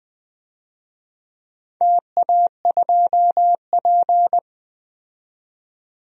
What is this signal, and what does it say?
Morse code, heard as a keyed tone, about 20 wpm, 700 Hz. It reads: TA2P
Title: DXCC-100 Call Signs - 4 Characters - 20wpm